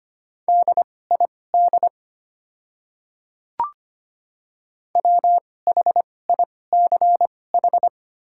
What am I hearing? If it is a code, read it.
Morse code, heard as a keyed tone, about 25 wpm, spaced as for 18 wpm, 700 Hz. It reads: DID WHICH
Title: 2x Character Spacing - Top 500 Words - 25wpm